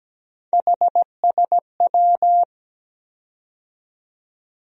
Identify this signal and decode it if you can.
Morse code, heard as a keyed tone, about 17 wpm, 700 Hz. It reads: HSW
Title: USA Call Sign Postfixes - 3 Letters 17wpm